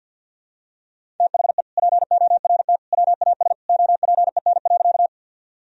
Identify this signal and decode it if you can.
Morse code, heard as a keyed tone, about 50 wpm, 700 Hz. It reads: THE PORT WAS OPEN.
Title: Sentences from Top 1000 Words - Review Part 1 - 50wpm